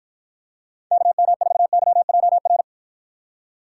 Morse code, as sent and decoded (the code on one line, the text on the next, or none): -.- -- ....- -.-- .--- .-.
KM4YJR